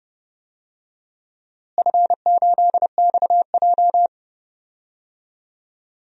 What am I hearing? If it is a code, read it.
Morse code, heard as a keyed tone, about 30 wpm, 700 Hz. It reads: F8XJ
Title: DXCC-100 Call Signs - 4 Characters - Repeated 2x - 30wpm